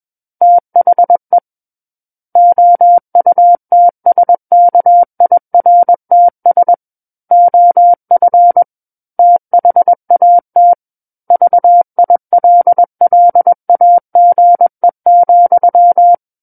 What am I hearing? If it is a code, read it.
Morse code, heard as a keyed tone, about 21 wpm, 700 Hz. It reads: THE OUTSKIRTS OF THAT VILLAGE,
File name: Book One / 12-Chapter 12 - 21 WPM